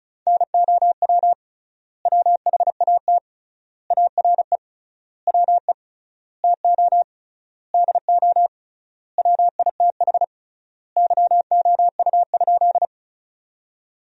Morse code, heard as a keyed tone, about 36 wpm, 700 Hz. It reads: NOW WHAT ARE WE TO DO WITH YOU?
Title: 3x Word-Spacing - Sentences from Top 100 Words - 35wpm